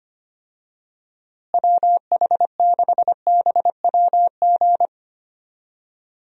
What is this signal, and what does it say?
Morse code, heard as a keyed tone, about 25 wpm, 700 Hz. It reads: WH6BWG
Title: USA Call Signs - 6 Characters - Repeated 3x - 25wpm